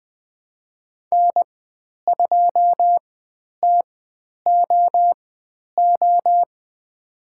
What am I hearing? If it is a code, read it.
Morse code, heard as a keyed tone, about 20 wpm, spaced as for 10 wpm, 700 Hz. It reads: N2TOO
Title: 095 Lesson 16g - Letter c - Call Signs 20wpm